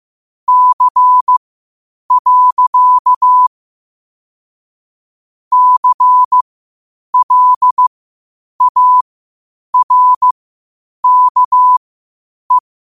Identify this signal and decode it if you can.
Morse code, heard as a keyed tone, about 15 wpm, spaced as for 8 wpm, 1000 Hz. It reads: C. CLARKE
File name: qotdfiles / 8wpm